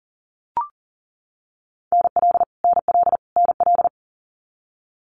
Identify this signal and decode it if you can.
Morse code, heard as a keyed tone, about 40 wpm, 700 Hz. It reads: NL NL NL
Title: Canadian Provinces and Territories - Repeated 3x - 40wpm